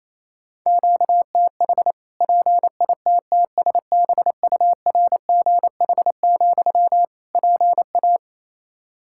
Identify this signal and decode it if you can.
Morse code, heard as a keyed tone, about 28 wpm, 700 Hz. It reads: QTH PITTSBURGH, PA
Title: Practice QSO - Exchange 1 - Beginning - 28wpm